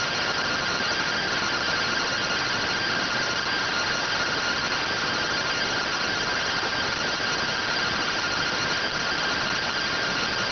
Two-tone teleprinter signal, about 170 Hz shift.